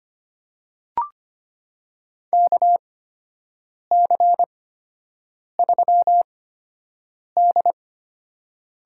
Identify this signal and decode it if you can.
Morse code, heard as a keyed tone, about 25 wpm, spaced as for 7 wpm, 700 Hz. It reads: KC3D
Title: USA Call Signs - 6 Characters - 8x Character Spacing - 25wpm